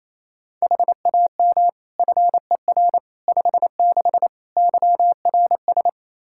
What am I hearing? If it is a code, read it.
Morse code, heard as a keyed tone, about 28 wpm, 700 Hz. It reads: HAM FER 56 YRS
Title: K1USN SST QSO Practice 28wpm